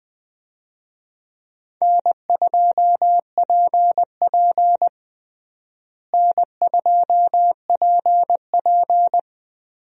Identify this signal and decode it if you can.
Morse code, heard as a keyed tone, about 20 wpm, 700 Hz. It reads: N2PP N2PP
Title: USA Call Signs - 4 Characters - Repeated 2x - 20wpm